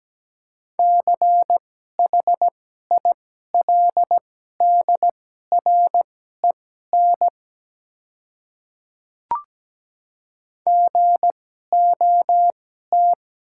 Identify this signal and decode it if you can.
Morse code, heard as a keyed tone, about 17 wpm, spaced as for 12 wpm, 700 Hz. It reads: CHILDREN GOT
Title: 2x Character Spacing - Top 300 Words - 17wpm